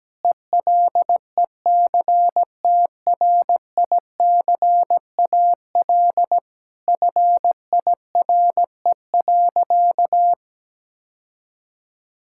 Morse code, heard as a keyed tone, about 17 wpm, 700 Hz. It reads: ELECTRICAL FIRE.
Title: Sentences from Top 1000 Words - Review Part 6 - 17wpm